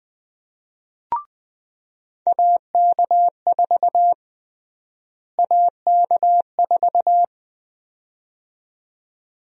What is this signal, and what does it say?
Morse code, heard as a keyed tone, about 21 wpm, 700 Hz. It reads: AK4 AK4
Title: USA Call Sign Prefixes - Repeated 2x - 20wpm